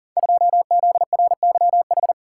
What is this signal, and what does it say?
Morse code, heard as a keyed tone, about 39 wpm, 700 Hz. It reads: 2ZRYH
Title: Code Groups - 5 Letter:Numbers - 40wpm